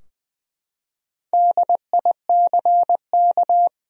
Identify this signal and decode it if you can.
Morse code, heard as a keyed tone, about 20 wpm, 700 Hz. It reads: DICK
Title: USA Common Names - 20wpm